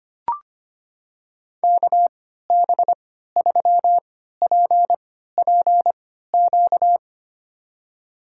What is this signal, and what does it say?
Morse code, heard as a keyed tone, about 25 wpm, spaced as for 14 wpm, 700 Hz. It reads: KB3PPQ
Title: USA Call Signs - 3x Character Spacing - 25wpm